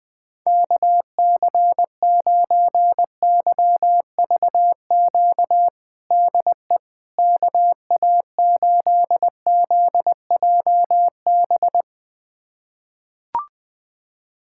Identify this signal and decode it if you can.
Morse code, heard as a keyed tone, about 21 wpm, 700 Hz. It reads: KC9YVQ DE KA8ZJB
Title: USA Callsign DE Callsign - 6 Characters - 20wpm